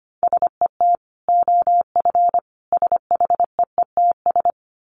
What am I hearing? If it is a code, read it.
Morse code, heard as a keyed tone, about 25 wpm, 700 Hz. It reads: SET OF SHEETS
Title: Dit Challenge 25wpm